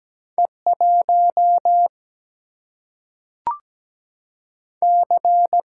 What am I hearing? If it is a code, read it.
Morse code, heard as a keyed tone, about 17 wpm, 700 Hz. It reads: E1 C
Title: Single-Letter Single-Number 17wpm